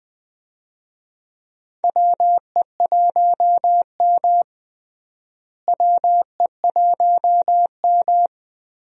Morse code, heard as a keyed tone, about 20 wpm, 700 Hz. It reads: WE1M WE1M
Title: USA Call Signs - 4 Characters - Repeated 3x - 20wpm